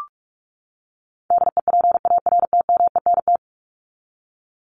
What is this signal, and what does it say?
Morse code, heard as a keyed tone, about 44 wpm, 700 Hz. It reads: DEPARTMENT
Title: Top 751-1000 Words 45wpm